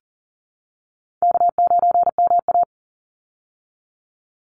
Morse code, heard as a keyed tone, about 40 wpm, 700 Hz. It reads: K9MA
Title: Call Signs Repeated 3x - 40wpm